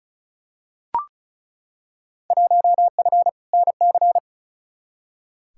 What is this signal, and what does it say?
Morse code, heard as a keyed tone, about 36 wpm, 700 Hz. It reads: 1F NC
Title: Field Day Exchanges - 35wpm